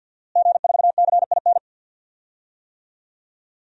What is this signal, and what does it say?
Morse code, heard as a keyed tone, about 47 wpm, 700 Hz. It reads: G4CIN
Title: DXCC-100 Call Signs - 5 Characters - Repeated 2x - 50wpm